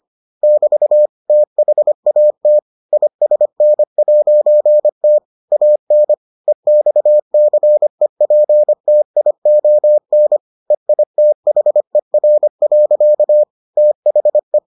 Morse code, heard as a keyed tone, about 25 wpm, 600 Hz. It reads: <BT> THAT ISN'T AN EXCEPTION EITHER. THE